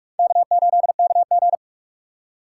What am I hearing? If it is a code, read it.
Morse code, heard as a keyed tone, about 46 wpm, 700 Hz. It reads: K8KG
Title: USA Call Signs - 4 Characters - 3x Character Spacing - 45wpm